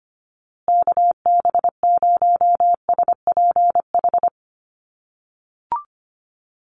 Morse code, heard as a keyed tone, about 25 wpm, 700 Hz. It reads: KB0SPH
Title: USA Call Signs Repeated 2x 25wpm